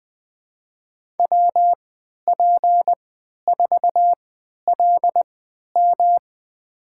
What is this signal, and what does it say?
Morse code, heard as a keyed tone, about 20 wpm, spaced as for 11 wpm, 700 Hz. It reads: WP4LM
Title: USA Call Signs - 3x Character Spacing - 20wpm